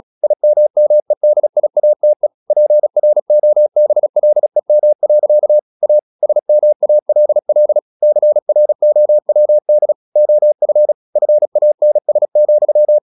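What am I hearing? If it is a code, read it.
Morse code, heard as a keyed tone, about 36 wpm, 600 Hz. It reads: IMMEDIATE PROBLEM. A SMALL CROWD OF FANS,
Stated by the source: Morse practice at 36 wpm, text of Ken Follett's novel Never, from CW med Gnister.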